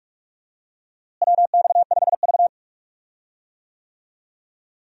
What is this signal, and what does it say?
Morse code, heard as a keyed tone, about 45 wpm, 700 Hz. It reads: WX5V
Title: USA Call Signs - 4 Characters - Repeated 2x - 45wpm